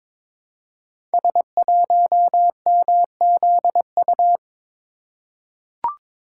Morse code, heard as a keyed tone, about 22 wpm, 700 Hz. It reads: S1MZU
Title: Code Groups - 5 Letter:Numbers - 22wpm